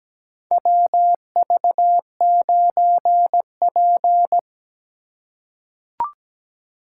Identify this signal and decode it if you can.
Morse code, heard as a keyed tone, about 17 wpm, 700 Hz. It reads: WV9P
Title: USA Call Signs - 4 Characters - 3x Character Spacing - 17wpm